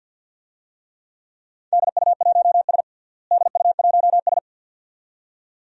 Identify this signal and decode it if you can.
Morse code, heard as a keyed tone, about 46 wpm, 700 Hz. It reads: DU1S DU1S
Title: DXCC-100 Call Signs - 4 Characters - Repeated 2x - 50wpm